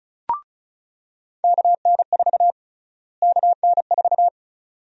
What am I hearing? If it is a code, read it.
Morse code, heard as a keyed tone, about 36 wpm, 700 Hz. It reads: KN4 KN4
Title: USA Call Sign Prefixes - Repeated 2x - 35wpm